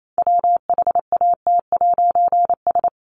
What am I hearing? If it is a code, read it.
Morse code, heard as a keyed tone, about 28 wpm, 700 Hz. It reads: WHAT'S
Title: A Sound Of Thunder 28wpm